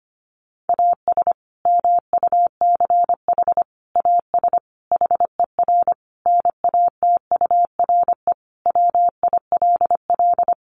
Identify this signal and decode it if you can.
Morse code, heard as a keyed tone, about 25 wpm, 700 Hz. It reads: AS MUCH AS HER NATURE WILL
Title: Sentences from Top 700 Words 25wpm